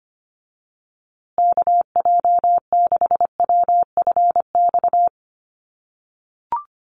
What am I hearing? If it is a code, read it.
Morse code, heard as a keyed tone, about 25 wpm, 700 Hz. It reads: KJ6WFX
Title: USA Call Signs Repeated 3x 25wpm